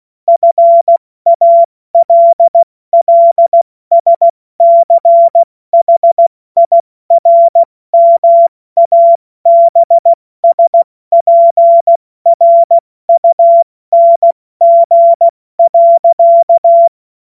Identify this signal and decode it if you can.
Morse code, heard as a keyed tone, about 16 wpm, 680 Hz. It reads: FALLSCHIRMABSPRUNG.
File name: Morsekurs / Witze / Witz 07. Dezember 2021, 70 F80